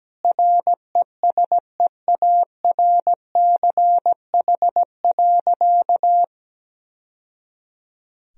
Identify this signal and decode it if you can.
Morse code, heard as a keyed tone, about 17 wpm, 700 Hz. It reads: RESEARCH.
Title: Sentences from Top 1000 Words - Review Part 4 - 17wpm